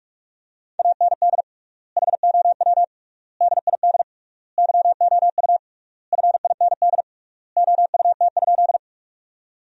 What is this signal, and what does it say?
Morse code, heard as a keyed tone, about 46 wpm, 700 Hz. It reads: AND HOW DID YOU FIND OUT?
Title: 3x Word-Spacing - Sentences from Top 200 Words - 45wpm